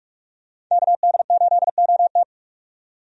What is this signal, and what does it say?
Morse code, heard as a keyed tone, about 44 wpm, 700 Hz. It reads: KD8OT
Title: USA Call Signs - 5 Characters - 45wpm